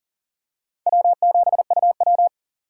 Morse code, heard as a keyed tone, about 40 wpm, 700 Hz. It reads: W7UW
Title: USA Call Signs - 4 Characters - 3x Character Spacing - 40wpm